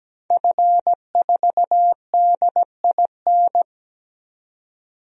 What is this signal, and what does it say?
Morse code, heard as a keyed tone, about 17 wpm, 700 Hz. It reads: F4DIN
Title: DXCC-100 Call Signs - 5 Characters - 17wpm